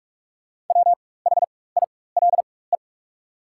Morse code, heard as a keyed tone, about 44 wpm, spaced as for 21 wpm, 690 Hz. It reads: WHILE
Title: 4x Character Spacing - Top 500 Words - 45wpm